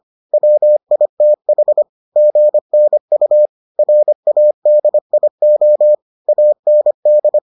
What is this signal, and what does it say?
Morse code, heard as a keyed tone, about 25 wpm, 600 Hz. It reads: WITH GNU RADIO AND